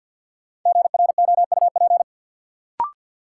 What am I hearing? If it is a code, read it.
Morse code, heard as a keyed tone, about 52 wpm, 700 Hz. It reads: GROUP
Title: Sentences from Top 1000 Words - Review Part 1 - New Word Drill - 50wpm